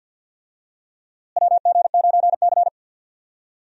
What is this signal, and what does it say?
Morse code, heard as a keyed tone, about 48 wpm, 700 Hz. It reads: WG9C
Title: USA Call Signs - 4 Characters - Repeated 3x - 50wpm